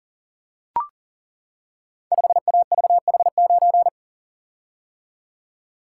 Morse code, heard as a keyed tone, about 39 wpm, 700 Hz. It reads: 5AVH9
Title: Code Groups - 5 Letter:Numbers - 40wpm